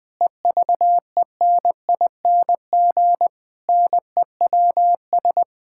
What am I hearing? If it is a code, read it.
Morse code, heard as a keyed tone, about 20 wpm, 700 Hz. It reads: EVENING NEWS
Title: Sets of 9 Words - 20wpm